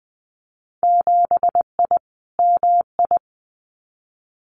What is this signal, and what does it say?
Morse code, heard as a keyed tone, about 19 wpm, 700 Hz. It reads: 7I MI
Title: Winter Field Day Exchanges Encore 20wpm